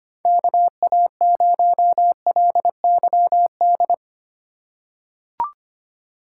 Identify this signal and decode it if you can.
Morse code, heard as a keyed tone, about 25 wpm, 700 Hz. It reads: KA0LYD
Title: USA Call Signs - 6 Characters - 3x Character Spacing - 25wpm